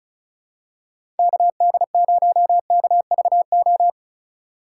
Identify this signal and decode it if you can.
Morse code, heard as a keyed tone, about 34 wpm, 690 Hz. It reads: KD0KVO
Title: USA Call Signs - 6 Characters - 4x Character Spacing - 35wpm